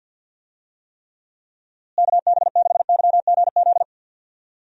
Morse code, heard as a keyed tone, about 49 wpm, 700 Hz. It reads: KB6YGZ
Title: USA Call Signs - 6 Characters - Repeated 2x - 50wpm